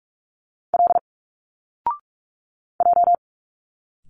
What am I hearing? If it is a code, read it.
Morse code, heard as a keyed tone, about 46 wpm, 700 Hz. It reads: L J
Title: Single Letters 45wpm